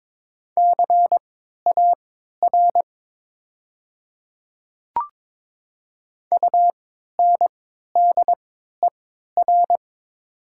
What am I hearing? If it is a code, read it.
Morse code, heard as a keyed tone, about 22 wpm, spaced as for 12 wpm, 700 Hz. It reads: CAR UNDER
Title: 3x Character Spacing - Top 300 Words - 22wpm